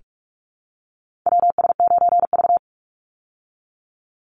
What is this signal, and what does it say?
Morse code, heard as a keyed tone, about 44 wpm, 700 Hz. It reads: WS9V
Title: Call Signs Repeated 2x - 45wpm